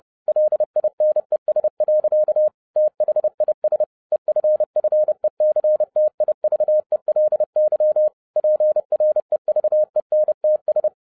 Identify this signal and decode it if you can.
Morse code, heard as a keyed tone, about 30 wpm, 610 Hz. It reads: LINES. THIS EFFECTIVELY PREVENTS
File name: 2012-01-mp3 / n12-I DIDNT MISS THE SOLAR CYCLE 24-30wpm